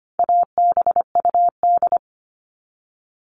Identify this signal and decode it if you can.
Morse code, heard as a keyed tone, about 25 wpm, 700 Hz. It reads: ABUD
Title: All Cut Numbers - 4-Digit 25wpm